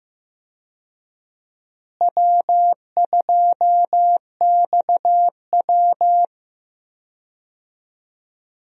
Morse code, heard as a keyed tone, about 15 wpm, 700 Hz. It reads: W2XW
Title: USA Call Signs - 4 Characters - Repeated 3x - 15wpm